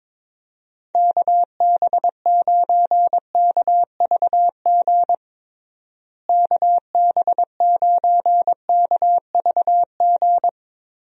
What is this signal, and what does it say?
Morse code, heard as a keyed tone, about 22 wpm, 700 Hz. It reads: KB9KVG KB9KVG
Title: USA Call Signs - 6 Characters - Repeated 3x - 22wpm